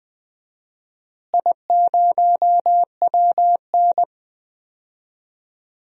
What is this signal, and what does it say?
Morse code, heard as a keyed tone, about 21 wpm, 700 Hz. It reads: I0WN
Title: DXCC-100 Call Signs - 4 Characters - Repeated 3x - 20wpm